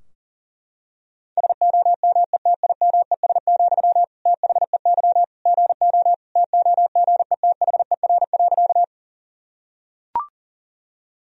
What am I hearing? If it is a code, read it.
Morse code, heard as a keyed tone, about 37 wpm, 700 Hz. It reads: SOMETIMES, THEY GO TOGETHER.
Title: Sentences from Top 1000 Words - Review Part 3 - 40wpm